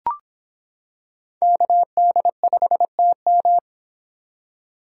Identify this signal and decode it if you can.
Morse code, heard as a keyed tone, about 25 wpm, 700 Hz. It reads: KD5TM